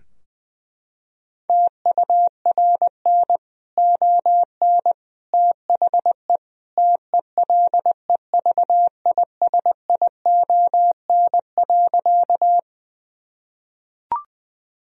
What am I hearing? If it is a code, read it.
Morse code, heard as a keyed tone, about 20 wpm, 700 Hz. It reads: TURN ON THE TELEVISION.
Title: Sentences from Top 1000 Words - Review Part 4 - 20wpm